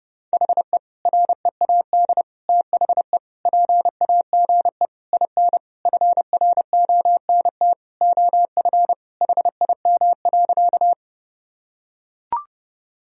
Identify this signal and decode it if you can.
Morse code, heard as a keyed tone, about 30 wpm, 700 Hz. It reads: HE READ THE PAGE IN FRONT OF HIM.
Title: Sentences from Top 1000 Words - Review Part 1 - 30wpm